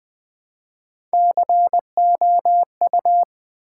Morse code, heard as a keyed tone, about 20 wpm, 700 Hz. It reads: COU
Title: Build A Word - 1 Letter At A Time - 5-Letter Words - 20wpm